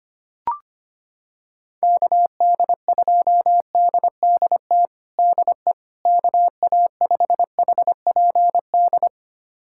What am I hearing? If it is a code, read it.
Morse code, heard as a keyed tone, about 25 wpm, 700 Hz. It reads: KD2DDT DE KA5HPD
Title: USA Callsign DE Callsign - 6 Characters - 25wpm